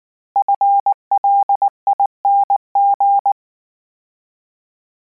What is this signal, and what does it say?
Morse code, heard as a keyed tone, about 19 wpm, 800 Hz. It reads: FLING
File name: Combinations / Combinations-mp3s / Combinations@18